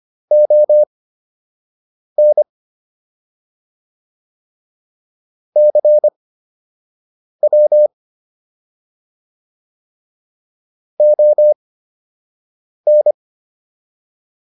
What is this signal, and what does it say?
Morse code, heard as a keyed tone, about 25 wpm, spaced as for 6 wpm, 600 Hz. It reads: ON CW ON